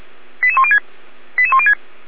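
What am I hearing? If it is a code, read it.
Five-tone selective call (ZVEI), heard as quick runs of five tones, tones of about 70 ms. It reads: 88106 88106
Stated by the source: ZVEI five-tone selective call received by SDR, from a Feuerwehr Orlamünde fire brigade server.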